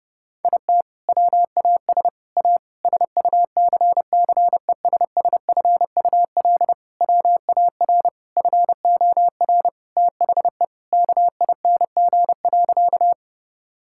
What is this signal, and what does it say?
Morse code, heard as a keyed tone, about 30 wpm, 700 Hz. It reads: IT WAS A SUCCESSFUL WAR FOR THE KING.
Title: Sentences from Top 1000 Words - Review Part 6 - 30wpm